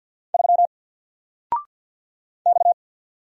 Morse code, heard as a keyed tone, about 50 wpm, 700 Hz. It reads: 3 X
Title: Single Letter-Number 50wpm